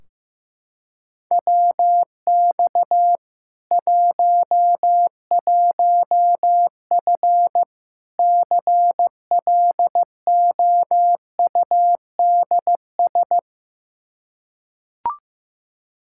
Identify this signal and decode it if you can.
Morse code, heard as a keyed tone, about 15 wpm, 700 Hz. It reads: WX 11F CLOUDS
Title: Practice QSO - Exchange 2 - Info - 15wpm